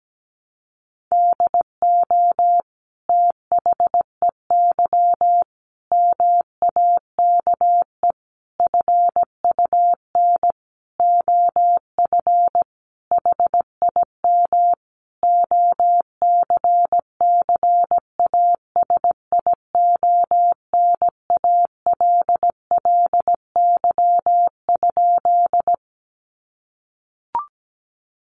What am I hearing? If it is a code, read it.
Morse code, heard as a keyed tone, about 17 wpm, 700 Hz. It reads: DO THEY MAKE FUN OF HIM OCCASIONALLY?
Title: Sentences from Top 1300 Words 17wpm